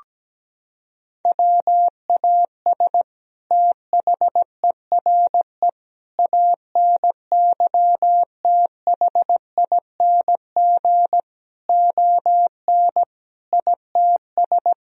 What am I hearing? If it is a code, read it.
Morse code, heard as a keyed tone, about 17 wpm, 700 Hz. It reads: WAS THERE ANYTHING ON ITS
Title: Sentences from Top 1000 Words - Review Part 3 - 17wpm